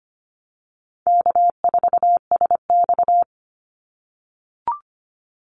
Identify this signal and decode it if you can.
Morse code, heard as a keyed tone, about 25 wpm, 700 Hz. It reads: K4SX
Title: Easy Call Signs - 25wpm